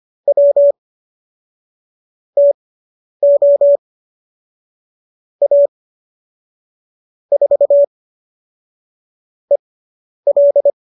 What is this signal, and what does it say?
Morse code, heard as a keyed tone, about 25 wpm, spaced as for 10 wpm, 570 Hz. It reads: W TO A 4 EL